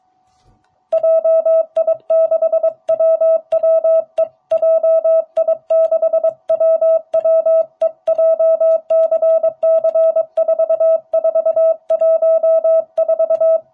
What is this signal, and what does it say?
Morse code, heard as a keyed tone, about 22 wpm, 660 Hz. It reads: JI6WWE JI6WWE JCC4414